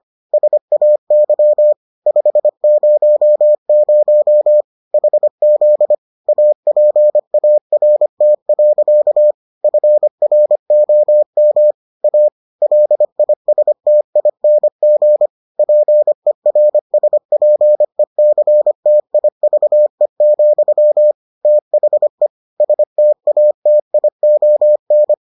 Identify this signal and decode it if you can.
Morse code, heard as a keyed tone, about 25 wpm, 600 Hz. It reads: SAY 500 HZ APART. FROM A LISTING PERSPECTIVE, THE STATION